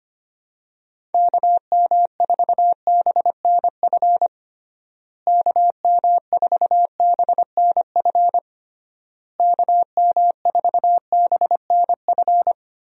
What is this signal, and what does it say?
Morse code, heard as a keyed tone, about 25 wpm, 700 Hz. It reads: KM4BNF KM4BNF KM4BNF
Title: USA Call Signs - 6 Characters - Repeated 3x - 25wpm